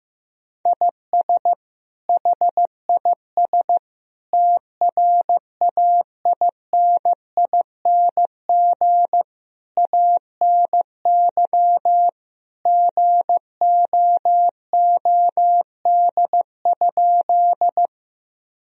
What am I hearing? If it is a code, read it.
Morse code, heard as a keyed tone, about 16 wpm, 700 Hz. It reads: IS HIS TRAINING ANY GOOD?
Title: Sentences from Top 1000 Words - Review Part 4 - 15wpm